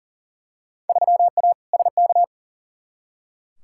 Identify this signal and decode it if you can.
Morse code, heard as a keyed tone, about 40 wpm, 700 Hz. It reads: 3A SK
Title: Field Day Exchanges Encore - 40wpm